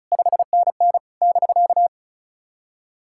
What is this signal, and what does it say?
Morse code, heard as a keyed tone, about 37 wpm, 700 Hz. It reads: 5NN <BK>
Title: SOTA QSO Practice 35wpm